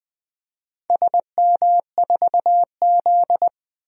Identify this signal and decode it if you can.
Morse code, heard as a keyed tone, about 21 wpm, 700 Hz. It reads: SM4Z
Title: DXCC-100 Call Signs - 4 Characters - 20wpm